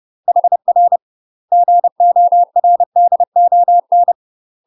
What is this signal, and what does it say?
Morse code, heard as a keyed tone, about 30 wpm, 700 Hz. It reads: HR GORDON
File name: UK MASTER QSOs TRAINING / 30WPM / UK MASTER QSOs 0042